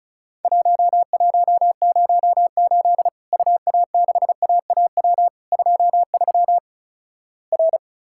Morse code, heard as a keyed tone, about 36 wpm, 700 Hz. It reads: 1108 UA6AAW 23 R
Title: Worked All Europe DX Contest - QTC Exchange - 35wpm